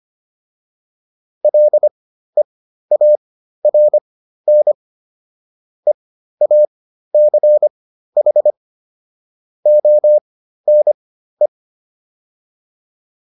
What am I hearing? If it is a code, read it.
Morse code, heard as a keyed tone, about 25 wpm, spaced as for 13 wpm, 600 Hz. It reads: LEARN EACH ONE
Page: intermediate-practice-files